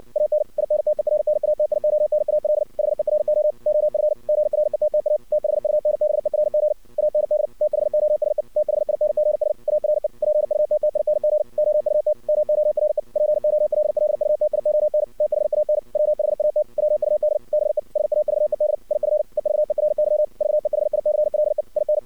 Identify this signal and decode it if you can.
Morse code, heard as a keyed tone, about 59 wpm, 610 Hz. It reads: AN INTERNATIONAL HERO OF GRETTA THUNBERG AND THOSE THINGS ARE PUTTING OUT MORE POLLUTION THAN WHAT WUD BE SAVED IF EVERY VEHICLE IN
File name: archived / CRSnet